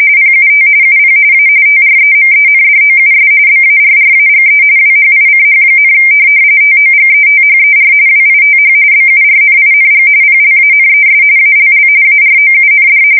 RTTY, about 45.45 baud. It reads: PUT UP A TREMENDOUS BARRAGE WHICH, TOGETHER WITH BURNING PLANES ALL AROUND
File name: fsk170 / USS-New-Jersey